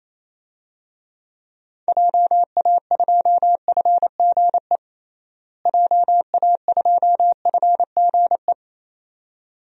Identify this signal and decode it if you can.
Morse code, heard as a keyed tone, about 28 wpm, 700 Hz. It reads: JA2FGE JA2FGE
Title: DXCC-100 Call Signs - 6 Characters - Repeated 2x - 28wpm